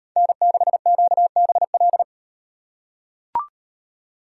Morse code, heard as a keyed tone, about 38 wpm, 700 Hz. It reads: N6QBL